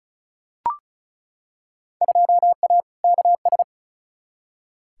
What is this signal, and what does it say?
Morse code, heard as a keyed tone, about 34 wpm, 700 Hz. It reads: 2A KS
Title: Field Day Exchanges - 35wpm